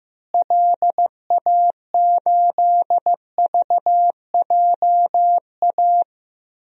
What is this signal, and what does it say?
Morse code, heard as a keyed tone, about 15 wpm, 700 Hz. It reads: LA8VJA
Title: DXCC-100 Call Signs - 6 Characters - Repeated 3x - 15wpm